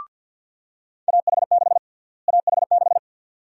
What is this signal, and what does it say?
Morse code, heard as a keyed tone, about 52 wpm, 700 Hz. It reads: AH6 AH6
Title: USA Call Sign Prefixes - Repeated 2x - 50wpm